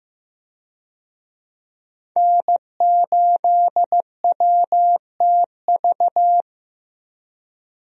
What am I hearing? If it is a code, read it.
Morse code, heard as a keyed tone, about 15 wpm, 700 Hz. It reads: N8WTV
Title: USA Call Signs - 5 Characters - Repeated 2x - 15wpm